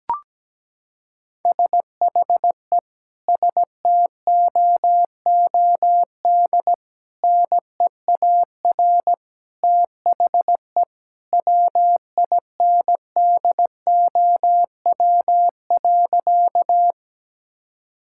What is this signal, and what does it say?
Morse code, heard as a keyed tone, about 17 wpm, 700 Hz. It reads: SHE STOOD NEAR THE WINDOW.
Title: Sentences from Top 1600 Words 17wpm